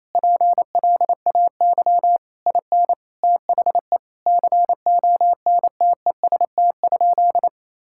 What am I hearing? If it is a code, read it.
Morse code, heard as a keyed tone, about 28 wpm, 700 Hz. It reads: PLAY IN THE CONTEST?
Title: Sentences from Top 1600 Words 28wpm